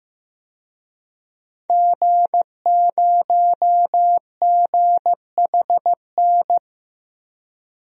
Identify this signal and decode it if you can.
Morse code, heard as a keyed tone, about 15 wpm, 700 Hz. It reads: G0GHN
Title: DXCC-100 Call Signs - 5 Characters - 15wpm